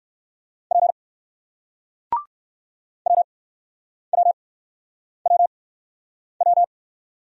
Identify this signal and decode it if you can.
Morse code, heard as a keyed tone, about 60 wpm, spaced as for 9 wpm, 700 Hz. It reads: P WWWW